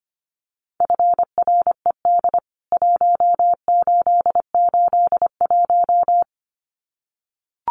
Morse code, heard as a keyed tone, about 25 wpm, 700 Hz. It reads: FRED 1881
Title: CWT Exchange 25wpm